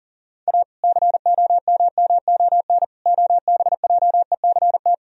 Morse code, heard as a keyed tone, about 40 wpm, 700 Hz. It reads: A COMMON OBJECT
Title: Sentences from Top 1000 Words - Review Part 3 - 40wpm